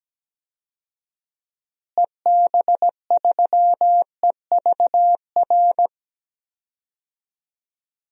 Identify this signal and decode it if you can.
Morse code, heard as a keyed tone, about 17 wpm, 700 Hz. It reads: EB3EVR
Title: DXCC-100 Call Signs - 6 Characters - Repeated 3x - 17wpm